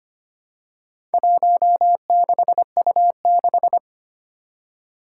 Morse code, heard as a keyed tone, about 25 wpm, 700 Hz. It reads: 16U6
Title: Four Challenging Letter-Number Combinations 25wpm